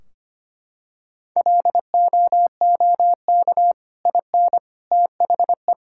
Morse code, heard as a keyed tone, about 25 wpm, 700 Hz. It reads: LOOK IN THE
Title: Sentences from Top 1600 Words 25wpm